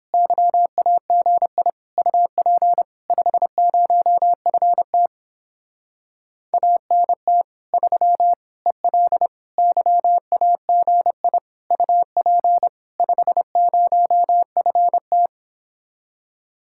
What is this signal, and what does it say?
Morse code, heard as a keyed tone, about 29 wpm, 700 Hz. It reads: YAGI UP 50FT ANT 3 EL YAGI UP 50FT